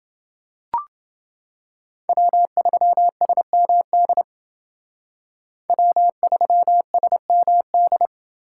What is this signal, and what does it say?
Morse code, heard as a keyed tone, about 28 wpm, 700 Hz. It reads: W3SMD W3SMD